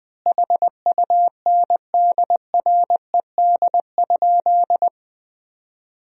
Morse code, heard as a keyed tone, about 20 wpm, 700 Hz. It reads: HUNDRED?
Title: 2x Word-Spacing - Sentences from Top 400 Words - 20wpm